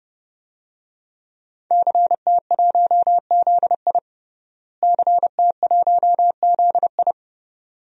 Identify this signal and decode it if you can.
Morse code, heard as a keyed tone, about 30 wpm, 700 Hz. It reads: CT1ZI CT1ZI
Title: DXCC-100 Call Signs - 5 Characters - Repeated 2x - 30wpm